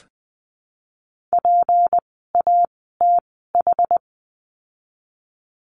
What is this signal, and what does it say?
Morse code, heard as a keyed tone, about 19 wpm, spaced as for 14 wpm, 700 Hz. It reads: PATH
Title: 153 Lesson 25f - Letter p - Words 20wpm